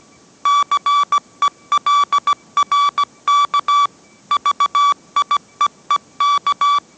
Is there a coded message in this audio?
Yes: Morse code from 0.5 to 6.8 s, heard as a keyed tone, about 17 wpm, 1150 Hz. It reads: CELRK VIEEK